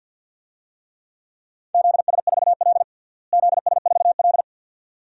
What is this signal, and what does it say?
Morse code, heard as a keyed tone, about 52 wpm, 700 Hz. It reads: ZS4L ZS4L
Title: DXCC-100 Call Signs - 4 Characters - Repeated 2x - 50wpm